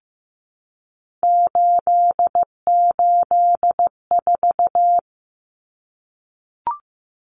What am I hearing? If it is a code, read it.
Morse code, heard as a keyed tone, about 15 wpm, 700 Hz. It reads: 884
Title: Common Cut Numbers - 3-Digit 15wpm